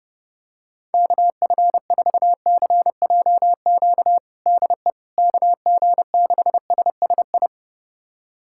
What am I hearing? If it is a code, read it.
Morse code, heard as a keyed tone, about 30 wpm, 700 Hz. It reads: KF4CJQ DE KG6SSI
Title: USA Callsign DE Callsign - 6 Characters - 30wpm